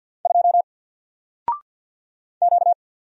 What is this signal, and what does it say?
Morse code, heard as a keyed tone, about 50 wpm, 700 Hz. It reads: 2 Q
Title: Single Letter-Number Alt 50wpm